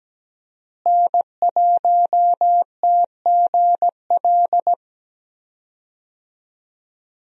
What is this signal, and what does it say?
Morse code, heard as a keyed tone, about 17 wpm, 700 Hz. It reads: N1TGL
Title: USA Call Signs - 5 Characters - Repeated 2x - 17wpm